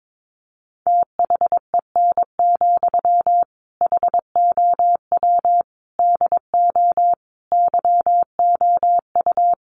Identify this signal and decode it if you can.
Morse code, heard as a keyed tone, about 22 wpm, 700 Hz. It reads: THEN, HOW DO YOU
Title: Sentences from Top 100 Words 22wpm